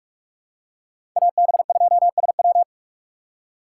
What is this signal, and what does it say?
Morse code, heard as a keyed tone, about 46 wpm, 700 Hz. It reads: AB2SW
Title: USA Call Signs - 5 Characters - 2x Character Spacing - 45wpm